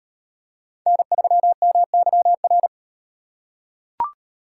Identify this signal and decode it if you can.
Morse code, heard as a keyed tone, about 38 wpm, 700 Hz. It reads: N3MYR